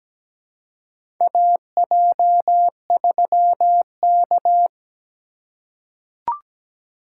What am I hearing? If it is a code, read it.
Morse code, heard as a keyed tone, about 17 wpm, 700 Hz. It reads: AJ3K
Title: USA Call Signs - 4 Characters - 4x Character Spacing - 17wpm